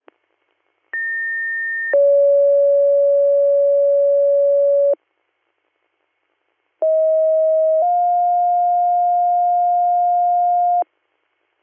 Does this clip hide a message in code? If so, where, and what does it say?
Yes: two-tone page from 0.9 to 10.8 s, heard as a one-second tone, then a three-second tone, about 1800 then 560 Hz, about 640 then 710 Hz.